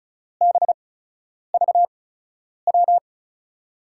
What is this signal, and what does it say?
Morse code, heard as a keyed tone, about 36 wpm, spaced as for 10 wpm, 700 Hz. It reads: BVW
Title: USA Call Signs - 6 Characters - 8x Character Spacing - 35wpm